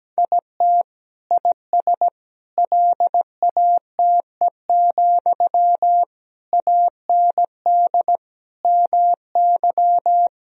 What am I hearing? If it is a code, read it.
Morse code, heard as a keyed tone, about 17 wpm, 700 Hz. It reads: IT IS LATE, AND MY
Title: Sentences from Top 1000 Words - Review Part 3 - 17wpm